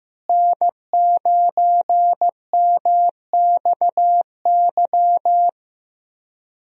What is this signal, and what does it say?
Morse code, heard as a keyed tone, about 15 wpm, 700 Hz. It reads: N9MXY
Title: USA Call Signs - 5 Characters - Repeated 3x - 15wpm